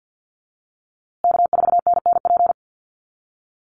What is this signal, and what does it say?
Morse code, heard as a keyed tone, about 50 wpm, 700 Hz. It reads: K4NNP
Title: USA Call Signs Repeated 3x 50wpm